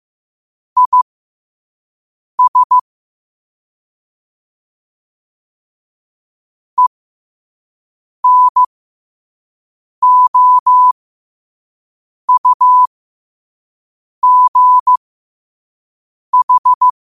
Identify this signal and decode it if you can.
Morse code, heard as a keyed tone, about 15 wpm, spaced as for 5 wpm, 1000 Hz. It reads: IS ENOUGH
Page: New quotes every day in morse code at 5 Words per minute.